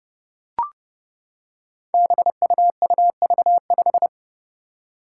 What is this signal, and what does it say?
Morse code, heard as a keyed tone, about 32 wpm, 700 Hz. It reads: BUUV5
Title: Five Challenging Letter-Number Combinations 30wpm